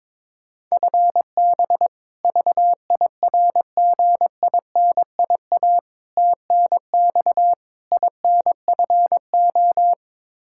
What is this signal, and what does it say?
Morse code, heard as a keyed tone, about 22 wpm, 700 Hz. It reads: FB VIRGINIA TNX INFO
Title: K1USN SST QSO Practice 22wpm